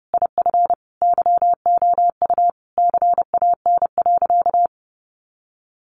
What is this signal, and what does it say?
Morse code, heard as a keyed tone, about 30 wpm, 700 Hz. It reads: IF YOU CAN.
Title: Sentences from Top 1500 Words 30wpm